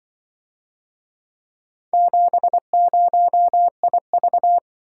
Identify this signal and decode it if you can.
Morse code, heard as a keyed tone, about 25 wpm, 700 Hz. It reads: 70IV
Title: Speed-Racing - Four Letter-Number Combinations - 20wpm